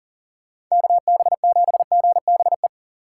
Morse code, heard as a keyed tone, about 40 wpm, 700 Hz. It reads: KB7GBE
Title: USA Call Signs - 2x Character Spacing - 40wpm